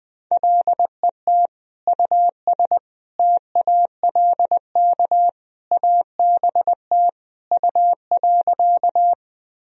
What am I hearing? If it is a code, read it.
Morse code, heard as a keyed tone, about 21 wpm, 700 Hz. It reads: LET US TALK ABT U.
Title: Sentences from Top 400 Words and Abbreviations 20wpm